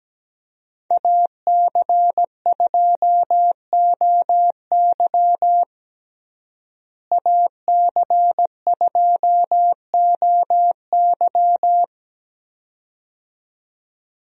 Morse code, heard as a keyed tone, about 17 wpm, 700 Hz. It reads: AC2OY AC2OY
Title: USA Call Signs - 5 Characters - Repeated 2x - 17wpm